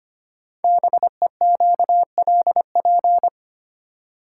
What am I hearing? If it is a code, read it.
Morse code, heard as a keyed tone, about 25 wpm, 700 Hz. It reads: BEQLP
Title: Code Groups - 5 Letters - 25wpm